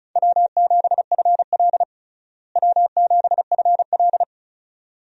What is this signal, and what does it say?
Morse code, heard as a keyed tone, about 36 wpm, 700 Hz. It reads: W7FL W7FL
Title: USA Call Signs - 4 Characters - Repeated 2x - 35wpm